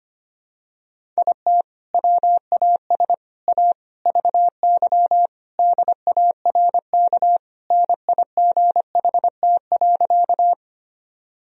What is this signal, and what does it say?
Morse code, heard as a keyed tone, about 25 wpm, 700 Hz. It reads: IT WAS A VY DARK NIGHT.
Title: Sentences from Top 500 Words and Abbreviations 25wpm